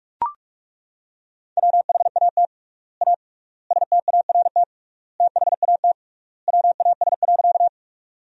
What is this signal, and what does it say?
Morse code, heard as a keyed tone, about 42 wpm, 700 Hz. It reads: WHAT A START THAT WAS.
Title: 3x Word-Spacing - Sentences from Top 300 Words - 45wpm